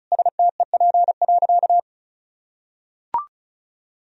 Morse code, heard as a keyed tone, about 36 wpm, 700 Hz. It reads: STEP.
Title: Sentences from Top 1000 Words - Review Part 2 - 35wpm